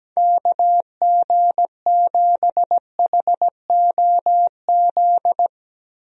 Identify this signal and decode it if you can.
Morse code, heard as a keyed tone, about 17 wpm, 700 Hz. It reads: KG7HOZ
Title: USA Call Signs - 6 Characters - Repeated 3x - 17wpm